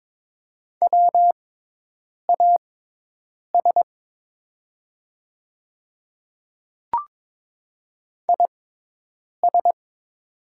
Morse code, heard as a keyed tone, about 22 wpm, spaced as for 8 wpm, 700 Hz. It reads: WAS IS
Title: 6x Character Spacing - Top 100 Words - 22wpm